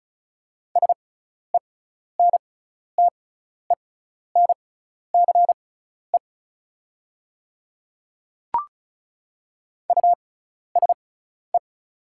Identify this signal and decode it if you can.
Morse code, heard as a keyed tone, about 36 wpm, spaced as for 12 wpm, 700 Hz. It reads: SENTENCE USE
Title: 6x Character Spacing - Top 200 Words - 35wpm